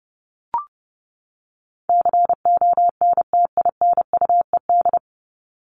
Morse code, heard as a keyed tone, about 30 wpm, 700 Hz. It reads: CONTINUED
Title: ED Suffix - Top 100 Words with Base Words 30wpm